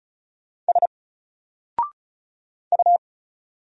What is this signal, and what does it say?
Morse code, heard as a keyed tone, about 33 wpm, 700 Hz. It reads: S U
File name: Single-Letter-Number-35wpm